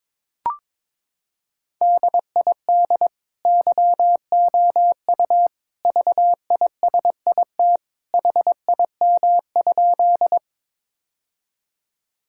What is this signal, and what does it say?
Morse code, heard as a keyed tone, about 22 wpm, 700 Hz. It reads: DID YOU VISIT HIM?
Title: Sentences from Top 1000 Words - Review Part 6 - 22wpm